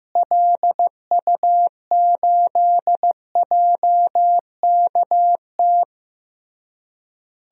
Morse code, heard as a keyed tone, about 15 wpm, 700 Hz. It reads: LU8JKT
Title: DXCC-100 Call Signs - 6 Characters - Repeated 3x - 15wpm